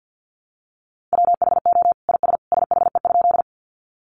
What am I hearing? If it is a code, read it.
Morse code, heard as a keyed tone, about 50 wpm, 700 Hz. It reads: WHO IS SHE?
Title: Sentences from Top 100 Words 50wpm